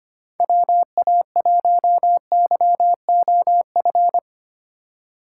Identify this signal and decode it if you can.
Morse code, heard as a keyed tone, about 25 wpm, 700 Hz. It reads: WA1YOF
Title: USA Call Signs - 6 Characters - 4x Character Spacing - 25wpm